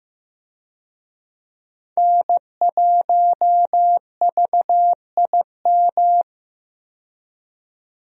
Morse code, heard as a keyed tone, about 15 wpm, 700 Hz. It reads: N1VIM
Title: USA Call Signs - 5 Characters - 15wpm